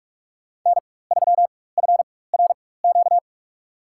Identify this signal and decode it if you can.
Morse code, heard as a keyed tone, about 47 wpm, spaced as for 21 wpm, 700 Hz. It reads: N3FRQ
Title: USA Call Signs - 4x Character Spacing - 45wpm